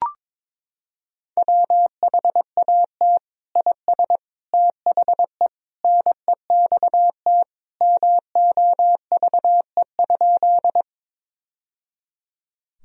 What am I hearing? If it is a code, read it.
Morse code, heard as a keyed tone, about 22 wpm, 700 Hz. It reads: WHAT IS THE NEXT MOVE?
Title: Sentences from Top 1000 Words - Review Part 1 - 22wpm